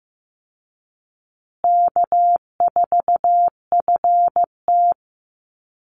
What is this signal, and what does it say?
Morse code, heard as a keyed tone, about 15 wpm, 700 Hz. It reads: K4FT
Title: Easy Call Signs Repeated 2x - 15wpm